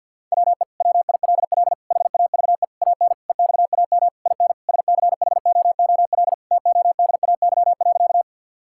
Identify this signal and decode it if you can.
Morse code, heard as a keyed tone, about 50 wpm, 700 Hz. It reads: WE WILL HAVE AN EXAM IN SCHOOL TODAY.
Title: Sentences from Top 1000 Words - Review Part 5 - 50wpm